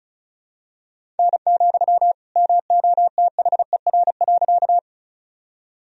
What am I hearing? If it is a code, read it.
Morse code, heard as a keyed tone, about 34 wpm, 700 Hz. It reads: N, MOTHER.
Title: Sentences from Top 200 Words and Abbreviations 35wpm